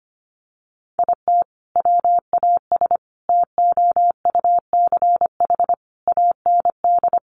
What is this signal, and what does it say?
Morse code, heard as a keyed tone, about 25 wpm, 700 Hz. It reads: IT WAS TOUCH AND
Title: Sentences from Top 1200 Words 25wpm